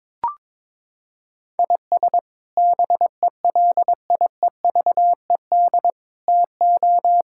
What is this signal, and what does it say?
Morse code, heard as a keyed tone, about 22 wpm, 700 Hz. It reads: IS BELIEVED TO
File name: practice / 22wpm / Sets of 6 Words - 22wpm